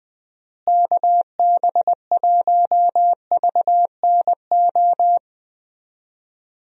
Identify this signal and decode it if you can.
Morse code, heard as a keyed tone, about 20 wpm, 700 Hz. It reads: KB1VNO
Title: USA Call Signs - 6 Characters - Repeated 3x - 20wpm